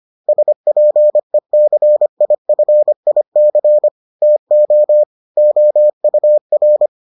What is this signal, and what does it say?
Morse code, heard as a keyed tone, about 25 wpm, 600 Hz. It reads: SPECIFIC TO OUR